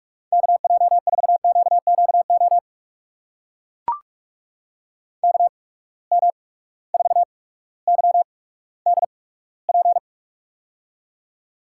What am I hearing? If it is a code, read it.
Morse code, heard as a keyed tone, about 44 wpm, spaced as for 19 wpm, 700 Hz. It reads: KJ4QQO KM4YDP
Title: USA Call Signs - 6 Characters - 8x Character Spacing - 45wpm